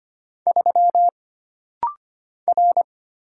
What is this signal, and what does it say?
Morse code, heard as a keyed tone, about 25 wpm, 700 Hz. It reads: 3 R
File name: Single Character (Letter:Number:Punctuation:Prosign) - Rapid-Fire - 25wpm